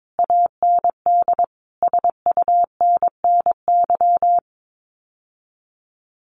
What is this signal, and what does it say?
Morse code, heard as a keyed tone, about 22 wpm, 700 Hz. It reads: AND SUNNY
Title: Common QSO Phrases Encore 22wpm